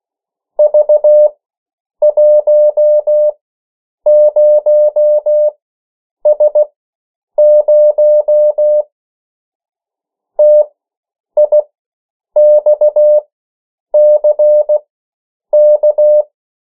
Morse code, heard as a keyed tone, about 16 wpm, spaced as for 8 wpm, 600 Hz. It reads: V10S0 TIXCK
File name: morse mp3 / 14 urok 13